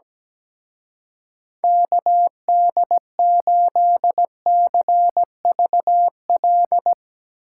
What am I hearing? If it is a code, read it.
Morse code, heard as a keyed tone, about 17 wpm, 700 Hz. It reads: KD8CVL
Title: Speed Racing - USA 2x3 Call Signs 17wpm